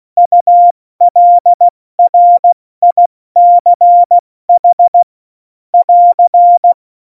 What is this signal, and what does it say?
Morse code, heard as a keyed tone, about 16 wpm, 700 Hz. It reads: ULRICH <AR>
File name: Morsekurs / Witze / Witz 05. Juni 2018, 70 F80